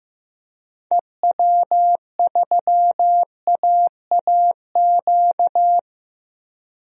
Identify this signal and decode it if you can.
Morse code, heard as a keyed tone, about 16 wpm, 700 Hz. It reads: EW3AAQ
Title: DXCC-100 Call Signs - 6 Characters - Repeated 2x - 15wpm